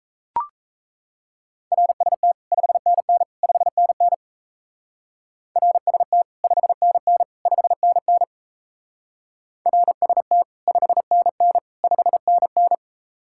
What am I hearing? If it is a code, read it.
Morse code, heard as a keyed tone, about 37 wpm, 700 Hz. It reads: RST 5NN 5NN RST 5NN 5NN RST 5NN 5NN